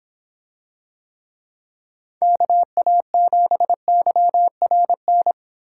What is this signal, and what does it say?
Morse code, heard as a keyed tone, about 26 wpm, 700 Hz. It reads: KA7YRN